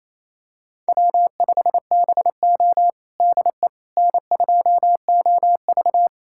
Morse code, heard as a keyed tone, about 27 wpm, 700 Hz. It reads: W5BO DE N2OV
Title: USA Callsign DE Callsign - 4 Characters - 28wpm